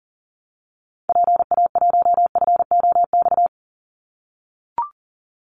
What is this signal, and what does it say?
Morse code, heard as a keyed tone, about 40 wpm, 700 Hz. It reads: PA1FOX
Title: Call Signs - 40wpm